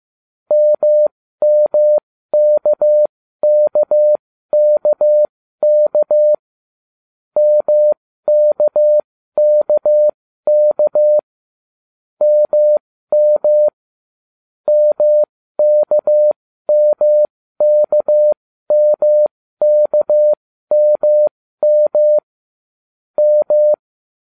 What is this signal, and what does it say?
Morse code, heard as a keyed tone, about 15 wpm, spaced as for 13 wpm, 600 Hz. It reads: MMKKKK MKKK MM MKMKMKMM M